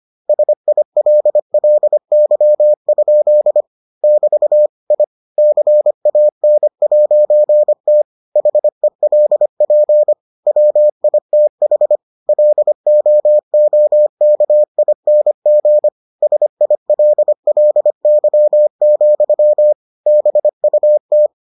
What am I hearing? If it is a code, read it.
Morse code, heard as a keyed tone, about 25 wpm, 600 Hz. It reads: SILLY? <BT> I CAN'T HELP WITH LOOKING SILLY, BUT